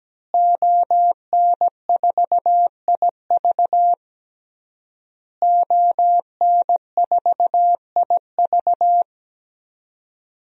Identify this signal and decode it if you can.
Morse code, heard as a keyed tone, about 17 wpm, 700 Hz. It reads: ON4IV ON4IV
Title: DXCC-100 Call Signs - 5 Characters - Repeated 3x - 17wpm